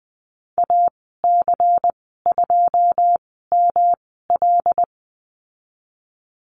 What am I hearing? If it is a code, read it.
Morse code, heard as a keyed tone, about 20 wpm, spaced as for 15 wpm, 700 Hz. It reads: AC2ML
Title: 122 Lesson 20g - Number 3 - Call Signs 20wpm